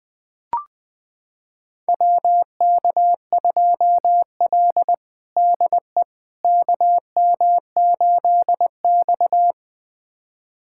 Morse code, heard as a keyed tone, about 20 wpm, 700 Hz. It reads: WK2L DE KM8X
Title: USA Callsign DE Callsign - 4 Characters - 20wpm